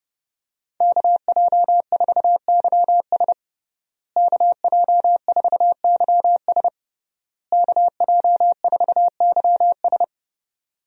Morse code, heard as a keyed tone, about 30 wpm, 700 Hz. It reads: KJ4YS KJ4YS KJ4YS
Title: USA Call Signs - 5 Characters - Repeated 3x - 30wpm